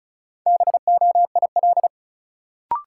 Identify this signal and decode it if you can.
Morse code, heard as a keyed tone, about 36 wpm, 700 Hz. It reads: BOIL
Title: Sets of 5 Words Encore - New Word Drill - 35wpm